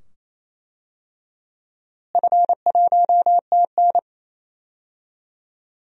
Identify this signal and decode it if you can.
Morse code, heard as a keyed tone, about 28 wpm, 700 Hz. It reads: F1TN
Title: DXCC-100 Call Signs - 4 Characters - Repeated 2x - 28wpm